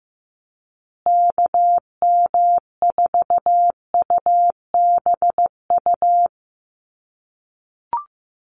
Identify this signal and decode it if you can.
Morse code, heard as a keyed tone, about 15 wpm, 700 Hz. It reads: KM4UBU
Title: USA Call Signs Repeated 3x 15wpm